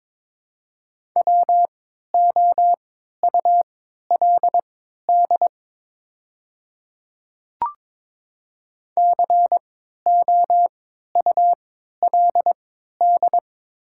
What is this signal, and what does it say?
Morse code, heard as a keyed tone, about 22 wpm, spaced as for 12 wpm, 700 Hz. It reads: WOULD COULD
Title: 3x Character Spacing - Top 300 Words - 22wpm